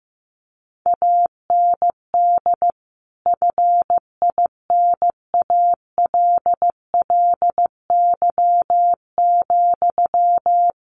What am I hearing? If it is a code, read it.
Morse code, heard as a keyed tone, about 15 wpm, 700 Hz. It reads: AND FINALLY,
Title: Sentences from Top 1100 Words 15wpm